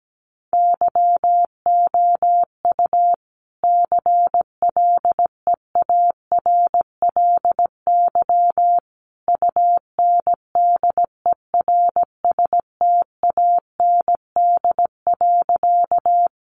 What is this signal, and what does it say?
Morse code, heard as a keyed tone, about 17 wpm, 700 Hz. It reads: YOU CLEARLY UNDERSTAND.
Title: Sentences from Top 1300 Words 17wpm